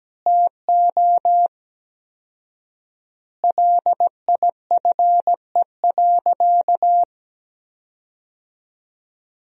Morse code, heard as a keyed tone, about 17 wpm, 700 Hz. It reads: TO LIFE.
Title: 4x Word-Spacing - Sentences from Top 400 Words - 17wpm